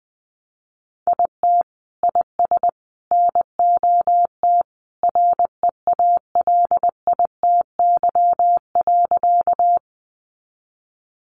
Morse code, heard as a keyed tone, about 20 wpm, 700 Hz. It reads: IT IS NOT REALITY.
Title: Sentences from Top 1200 Words 20wpm